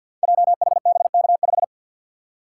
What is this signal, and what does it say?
Morse code, heard as a keyed tone, about 51 wpm, 700 Hz. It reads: JHBK5
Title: Code Groups - 5 Letter:Numbers - 50wpm